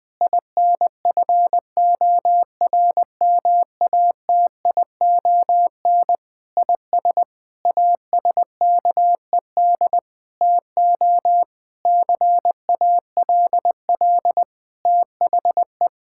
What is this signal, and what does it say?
Morse code, heard as a keyed tone, about 20 wpm, 700 Hz. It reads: INFORMATION IS ASKED TO CALL THE
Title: Sentences with 7-Word Phrases - 20wpm